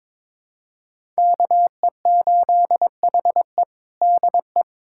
Morse code, heard as a keyed tone, about 22 wpm, 700 Hz. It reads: KE8HE DE
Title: USA Callsign DE Callsign - 5 Characters - 22wpm